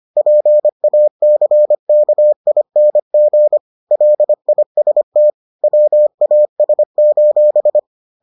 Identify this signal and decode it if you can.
Morse code, heard as a keyed tone, about 25 wpm, 600 Hz. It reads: PACKING LIST WAS: